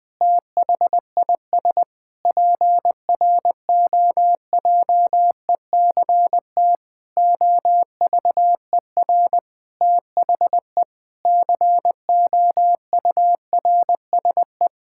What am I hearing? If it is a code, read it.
Morse code, heard as a keyed tone, about 20 wpm, 700 Hz. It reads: THIS PROJECT OVER THE COURSE
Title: Sentences with 6-Word Phrases - 20wpm